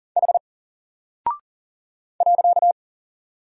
Morse code, heard as a keyed tone, about 40 wpm, 700 Hz. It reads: H .
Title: Single Character (Letter:Number:Punctuation:Prosign) 40wpm